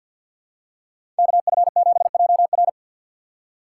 Morse code, heard as a keyed tone, about 47 wpm, 690 Hz. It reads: KF7JR
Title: USA Call Signs - 4x Character Spacing - 50wpm